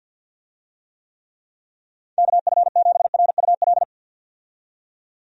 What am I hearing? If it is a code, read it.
Morse code, heard as a keyed tone, about 51 wpm, 700 Hz. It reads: KF7RUL
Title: USA Call Signs - 6 Characters - 50wpm